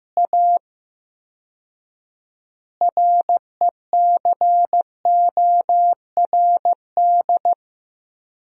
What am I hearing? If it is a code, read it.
Morse code, heard as a keyed tone, about 15 wpm, 700 Hz. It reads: A RECORD
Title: 4x Word-Spacing - Sentences from Top 500 Words - 15wpm